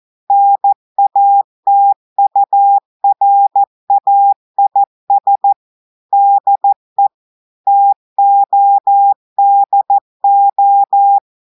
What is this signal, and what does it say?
Morse code, heard as a keyed tone, about 14 wpm, 800 Hz. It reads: NATURAIS DE TODO
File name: audios / portugais / portugais texte en clair 2 vitesse 14 mots minutes